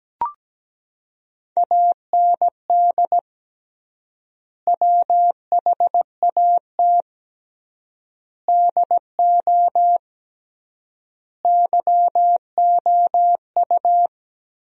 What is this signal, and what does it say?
Morse code, heard as a keyed tone, about 17 wpm, 700 Hz. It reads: AND WHAT DO YOU
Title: 3x Word-Spacing - Sentences from Top 100 Words - 17wpm